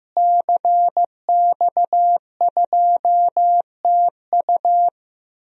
Morse code, heard as a keyed tone, about 15 wpm, 700 Hz. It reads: CX2TU
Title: DXCC-100 Call Signs - 5 Characters - Repeated 2x - 15wpm